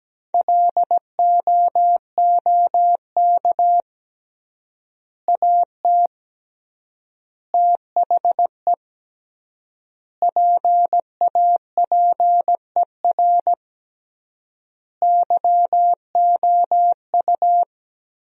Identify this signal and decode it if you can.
Morse code, heard as a keyed tone, about 17 wpm, 700 Hz. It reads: LOOK AT THE PAPER YOU
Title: 3x Word-Spacing - Sentences from Top 400 Words - 17wpm